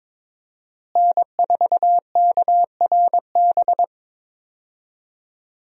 Morse code, heard as a keyed tone, about 22 wpm, 700 Hz. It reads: N4KRB
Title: USA Call Signs - 5 Characters - Repeated 2x - 22wpm